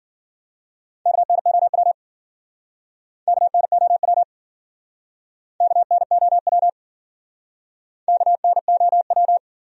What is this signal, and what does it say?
Morse code, heard as a keyed tone, about 50 wpm, 700 Hz. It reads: KNOW KNOW KNOW KNOW
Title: Speed-Racing Top 100 Words 40wpm